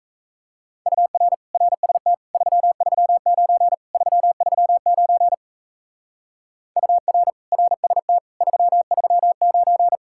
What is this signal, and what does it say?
Morse code, heard as a keyed tone, about 34 wpm, 700 Hz. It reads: UR RST 339 339 UR RST 339
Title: Speed-Racing Common QSO Phrases - 28wpm